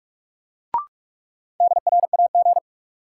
Morse code, heard as a keyed tone, about 44 wpm, 700 Hz. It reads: DRAG
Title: Sentences from Top 1500 Words - New Word Drill - 45wpm